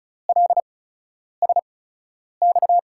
Morse code, heard as a keyed tone, about 36 wpm, spaced as for 10 wpm, 700 Hz. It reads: LSX
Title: USA Call Signs - 6 Characters - 8x Character Spacing - 35wpm